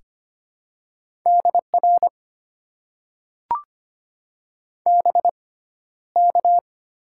Morse code, heard as a keyed tone, about 25 wpm, spaced as for 13 wpm, 700 Hz. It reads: DR BK
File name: Top 90 QSO Elements - 6x Character Spacing - 25wpm